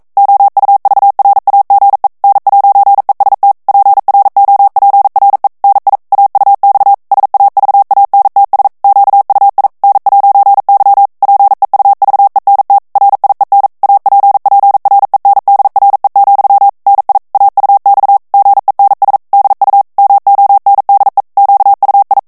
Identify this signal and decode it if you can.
Morse code, heard as a keyed tone, about 42 wpm, 800 Hz. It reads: OUVRAGE N'EST PROPRE NI AUX SAVANTS QUI N'Y PEUVENT RIEN APPRENDRE, NI AUX GENS DU MONDE QUI
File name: CW / FICHIERS-MP3 / FRANCAIS TEXTE CLAIR 2 VITESSE 40 MOTS MINUTES